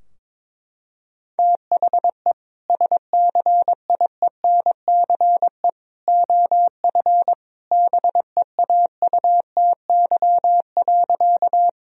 Morse code, heard as a keyed tone, about 22 wpm, 700 Hz. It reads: THE SCIENCE OF BEAUTY.
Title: Sentences from Top 1000 Words - Review Part 2 - 22wpm